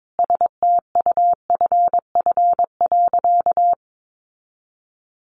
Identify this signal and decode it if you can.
Morse code, heard as a keyed tone, about 22 wpm, 700 Hz. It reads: STUFF.
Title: Sentences from Top 1400 Words 22wpm